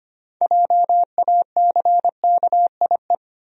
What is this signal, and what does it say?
Morse code, heard as a keyed tone, about 25 wpm, 700 Hz. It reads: JACKIE
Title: USA Common Names - 25wpm